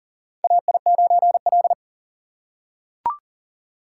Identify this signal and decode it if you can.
Morse code, heard as a keyed tone, about 40 wpm, 700 Hz. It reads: AI9L
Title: USA Call Signs - 2x Character Spacing - 40wpm